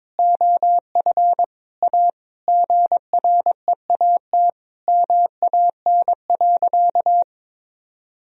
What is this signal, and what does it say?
Morse code, heard as a keyed tone, about 22 wpm, 700 Hz. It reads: OF A GREAT MAN.
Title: Sentences from Top 1000 Words - Review Part 2 - 22wpm